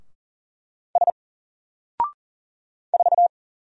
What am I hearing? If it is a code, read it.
Morse code, heard as a keyed tone, about 39 wpm, 700 Hz. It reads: S 4
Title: Single Character (Letter:Number:Punctuation:Prosign) Alt 40wpm